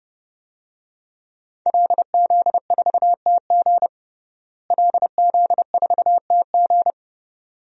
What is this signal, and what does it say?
Morse code, heard as a keyed tone, about 30 wpm, 700 Hz. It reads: LZ4TG LZ4TG
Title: DXCC-100 Call Signs - 5 Characters - Repeated 2x - 30wpm